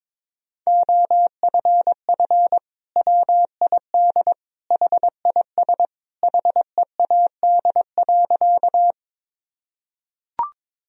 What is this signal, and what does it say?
Morse code, heard as a keyed tone, about 22 wpm, 700 Hz. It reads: OFF WID HIS HEAD.
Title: Sentences from Top 300 Words and Abbreviations 22wpm